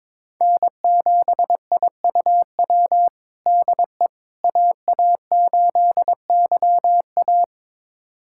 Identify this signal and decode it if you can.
Morse code, heard as a keyed tone, about 22 wpm, 700 Hz. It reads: N7IUW DE AA8YA
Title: USA Callsign DE Callsign - 5 Characters - 22wpm